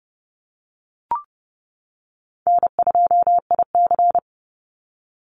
Morse code, heard as a keyed tone, about 30 wpm, 700 Hz. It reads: N2IC
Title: Easy Call Signs - 30wpm